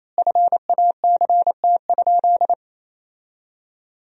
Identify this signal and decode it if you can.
Morse code, heard as a keyed tone, about 28 wpm, 700 Hz. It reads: FACT?
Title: 4x Word-Spacing - Sentences from Top 500 Words - 28wpm